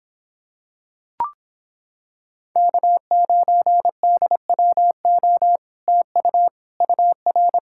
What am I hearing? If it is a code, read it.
Morse code, heard as a keyed tone, about 26 wpm, 700 Hz. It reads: K9DWO TU UR